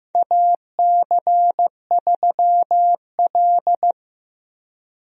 Morse code, heard as a keyed tone, about 15 wpm, 700 Hz. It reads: AC3L
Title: USA Call Signs - 4 Characters - Repeated 3x - 15wpm